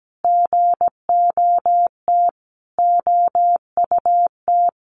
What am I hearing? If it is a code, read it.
Morse code, heard as a keyed tone, about 17 wpm, 700 Hz. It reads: GOT OUT
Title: A Sound Of Thunder 17wpm